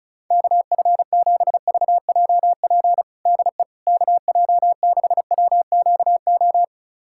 Morse code, heard as a keyed tone, about 34 wpm, 700 Hz. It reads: KF7VJP DE KJ6WQO
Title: USA Callsign DE Callsign 35wpm